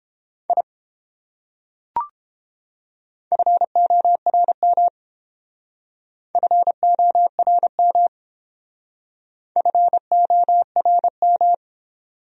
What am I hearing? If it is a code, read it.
Morse code, heard as a keyed tone, about 29 wpm, 700 Hz. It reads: I FORM FORM FORM